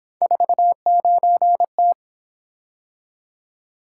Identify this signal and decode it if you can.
Morse code, heard as a keyed tone, about 26 wpm, 700 Hz. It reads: 49T
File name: Speed-Racing - Three Letter-Number Combinations - 22wpm